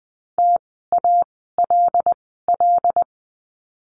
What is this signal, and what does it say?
Morse code, heard as a keyed tone, about 20 wpm, spaced as for 15 wpm, 700 Hz. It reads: TALL
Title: 075 Lesson 13f - Letter L - Words 20wpm